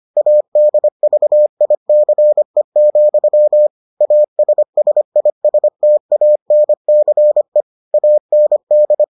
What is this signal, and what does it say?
Morse code, heard as a keyed tone, about 25 wpm, 600 Hz. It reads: ADVICE, ASSISTANCE AND